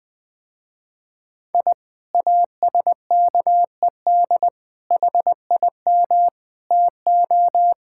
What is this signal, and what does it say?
Morse code, heard as a keyed tone, about 20 wpm, 700 Hz. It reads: I ASKED HIM TO
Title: Sentences with 2-Word Phrases - Spoken Beginning - 20wpm